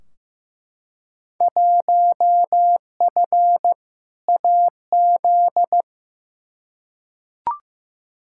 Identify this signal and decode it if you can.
Morse code, heard as a keyed tone, about 15 wpm, 700 Hz. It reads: 1F AZ
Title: Field Day Exchanges - 15wpm